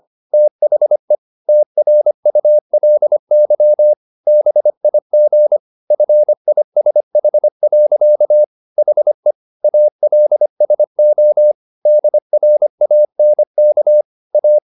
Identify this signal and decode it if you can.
Morse code, heard as a keyed tone, about 25 wpm, 600 Hz. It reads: THE TRULY BIG FISH. HE ALSO DRANK A
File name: the old man and the sea part 1 25wpm